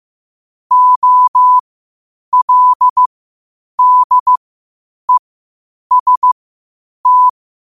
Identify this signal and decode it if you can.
Morse code, heard as a keyed tone, about 15 wpm, spaced as for 8 wpm, 1000 Hz. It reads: OLDEST
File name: qotdfiles / 8wpm